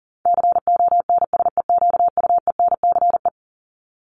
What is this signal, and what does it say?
Morse code, heard as a keyed tone, about 38 wpm, 700 Hz. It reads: CONSEQUENCE
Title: LY Suffix - Top 100 Words with Base Words 40wpm